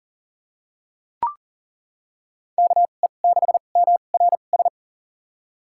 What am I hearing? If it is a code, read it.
Morse code, heard as a keyed tone, about 40 wpm, spaced as for 29 wpm, 700 Hz. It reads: KE6MRS
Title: USA Call Signs - 6 Characters - 2x Character Spacing - 40wpm